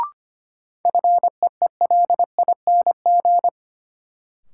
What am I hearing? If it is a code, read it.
Morse code, heard as a keyed tone, about 25 wpm, 700 Hz. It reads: FEELING
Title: Sentences from Top 1200 Words - New Word Drill Alt - 25wpm